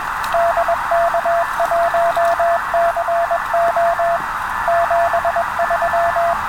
Morse code, heard as a keyed tone, about 21 wpm, 680 Hz, with radio noise. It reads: DK1CO 73